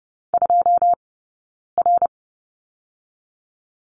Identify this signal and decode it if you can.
Morse code, heard as a keyed tone, about 30 wpm, 700 Hz. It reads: 2 R
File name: course / 30:10wpm / 099 Lesson 17d - Letter u - Two Characters 30wpm